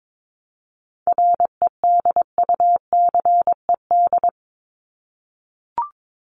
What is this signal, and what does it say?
Morse code, heard as a keyed tone, about 22 wpm, 700 Hz. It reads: REDUCED
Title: ED Suffix - Top 100 Words with Base Words 22wpm